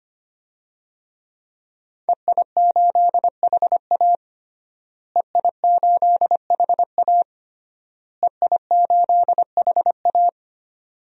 Morse code, heard as a keyed tone, about 25 wpm, 700 Hz. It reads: EI8HA EI8HA EI8HA
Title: DXCC-100 Call Signs - 5 Characters - Repeated 3x - 25wpm